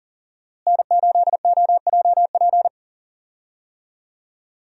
Morse code, heard as a keyed tone, about 39 wpm, 700 Hz. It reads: N8OJP
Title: USA Call Signs - 5 Characters - Repeated 3x - 40wpm